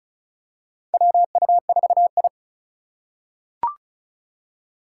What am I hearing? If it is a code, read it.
Morse code, heard as a keyed tone, about 34 wpm, 700 Hz. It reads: WU4I
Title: USA Call Signs - 4 Characters - 2x Character Spacing - 35wpm